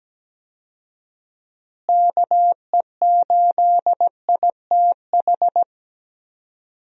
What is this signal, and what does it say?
Morse code, heard as a keyed tone, about 17 wpm, 700 Hz. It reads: KE8ITH
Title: USA Call Signs - 6 Characters - Repeated 2x - 17wpm